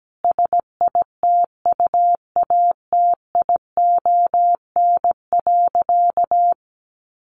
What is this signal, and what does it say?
Morse code, heard as a keyed tone, about 17 wpm, 700 Hz. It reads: SITUATION.
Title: Sentences from Top 1200 Words 17wpm